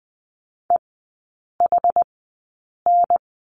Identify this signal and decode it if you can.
Morse code, heard as a keyed tone, about 20 wpm, spaced as for 8 wpm, 700 Hz. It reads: EHN
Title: 081 Lesson 14e - Number 2 - Three Characters 20wpm